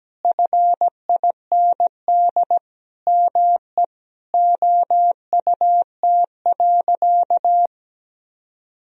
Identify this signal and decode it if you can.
Morse code, heard as a keyed tone, about 17 wpm, 700 Hz. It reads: FIND ME OUT.
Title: Sentences from Top 500 Words and Abbreviations 17wpm